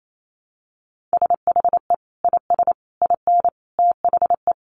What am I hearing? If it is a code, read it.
Morse code, heard as a keyed tone, about 28 wpm, 700 Hz. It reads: SHE IS IN THE
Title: Sentences from Top 1200 Words 28wpm